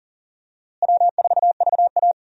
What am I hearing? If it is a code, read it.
Morse code, heard as a keyed tone, about 39 wpm, 700 Hz. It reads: W4VA
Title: USA Call Signs - 4 Characters - Repeated 3x - 40wpm